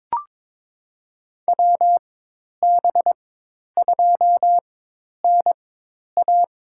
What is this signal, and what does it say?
Morse code, heard as a keyed tone, about 22 wpm, spaced as for 10 wpm, 700 Hz. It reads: WB2NA
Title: USA Call Signs - 4x Character Spacing - 22wpm